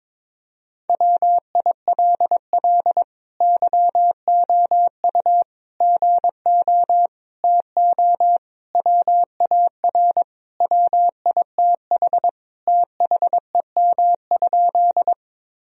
Morse code, heard as a keyed tone, about 22 wpm, 700 Hz. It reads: WILL YOU GO TO WAR WITH THEM?
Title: Sentences from Top 1000 Words - Review Part 2 - 22wpm